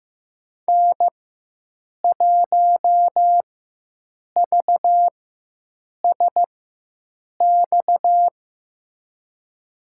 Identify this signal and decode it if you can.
Morse code, heard as a keyed tone, about 15 wpm, spaced as for 7 wpm, 700 Hz. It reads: N1VSX
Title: USA Call Signs - 5 Characters - 4x Character Spacing - 15wpm